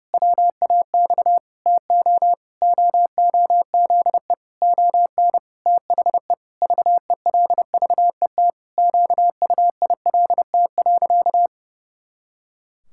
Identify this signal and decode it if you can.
Morse code, heard as a keyed tone, about 30 wpm, 700 Hz. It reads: WAX TO OOZE ON THE VELVET QUILT.
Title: Pangram Challenge v2 - 30wpm